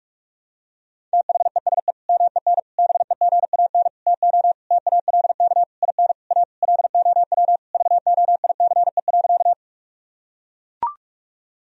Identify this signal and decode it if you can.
Morse code, heard as a keyed tone, about 41 wpm, 700 Hz. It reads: THESE MEN BEGAN TO TALK IN A LOW VOICE.
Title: Sentences from Top 1000 Words - Review Part 1 - 45wpm